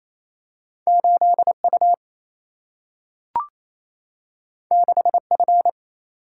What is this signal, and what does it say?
Morse code, heard as a keyed tone, about 28 wpm, 700 Hz. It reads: 8 U 6 F
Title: Single-Letter Single-Number 28wpm